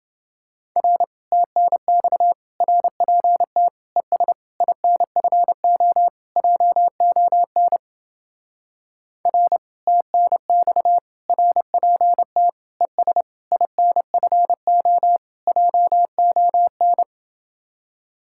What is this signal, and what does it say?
Morse code, heard as a keyed tone, about 30 wpm, 700 Hz. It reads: R TNX RPT ES INFO JON R TNX RPT ES INFO JON